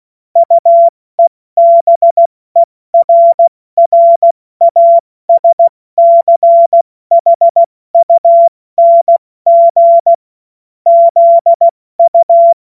Morse code, heard as a keyed tone, about 16 wpm, 680 Hz. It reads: UEBERRASCHUNG ZU
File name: Morsekurs / Witze / Witz 28. März 2023 70 F80